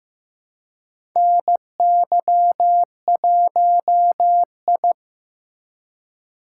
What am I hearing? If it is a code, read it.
Morse code, heard as a keyed tone, about 15 wpm, 700 Hz. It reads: NY1I
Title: USA Call Signs - 4 Characters - 4x Character Spacing - 15wpm